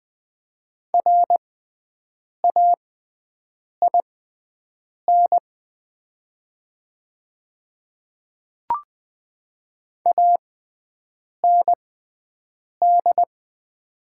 Morse code, heard as a keyed tone, about 20 wpm, spaced as for 7 wpm, 700 Hz. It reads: RAIN AND
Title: 6x Character Spacing - Top 300 Words - 20wpm